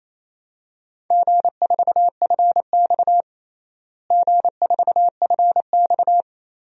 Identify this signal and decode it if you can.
Morse code, heard as a keyed tone, about 28 wpm, 700 Hz. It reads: G4FX G4FX
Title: DXCC-100 Call Signs - 4 Characters - Repeated 2x - 28wpm